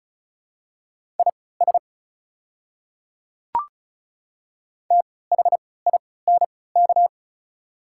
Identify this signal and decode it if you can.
Morse code, heard as a keyed tone, about 36 wpm, spaced as for 20 wpm, 700 Hz. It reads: IS THINK
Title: 3x Character Spacing - Top 200 Words - 35wpm